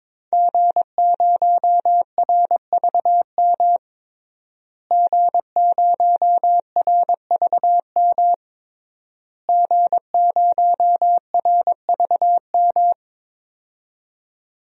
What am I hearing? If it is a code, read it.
Morse code, heard as a keyed tone, about 22 wpm, 700 Hz. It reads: G0RVM G0RVM G0RVM
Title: DXCC-100 Call Signs - 5 Characters - Repeated 3x - 22wpm